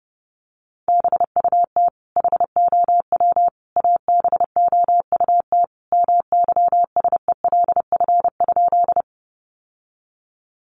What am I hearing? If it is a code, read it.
Morse code, heard as a keyed tone, about 30 wpm, 700 Hz. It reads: BUT HOW ABOUT MYSELF?
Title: Sentences from Top 1300 Words 30wpm